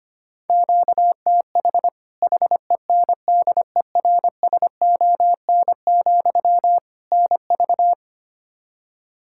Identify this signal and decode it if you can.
Morse code, heard as a keyed tone, about 25 wpm, 700 Hz. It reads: QTH HENDERSON, NV
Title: Practice QSO - Exchange 1 - Beginning - 25wpm